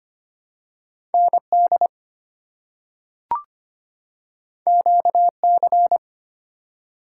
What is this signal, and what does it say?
Morse code, heard as a keyed tone, about 25 wpm, 700 Hz. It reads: ND QC
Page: morse-code-ninja-practice-25wpm